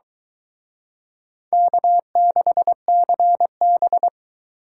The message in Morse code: K6CB

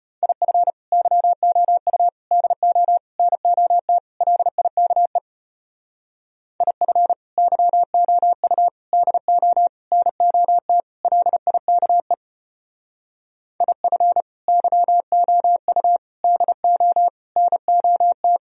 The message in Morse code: IF YOU DO NOT LIKE IF YOU DO NOT LIKE IF YOU DO NOT

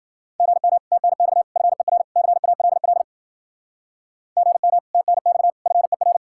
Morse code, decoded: GM TNX FER CALL GM TNX FER

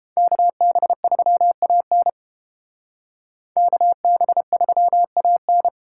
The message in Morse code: KB3AN KB3AN

